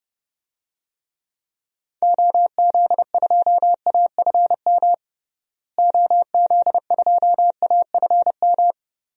Morse code, heard as a keyed tone, about 32 wpm, 700 Hz. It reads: OZ2AFM OZ2AFM
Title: DXCC-100 Call Signs - 6 Characters - Repeated 2x - 30wpm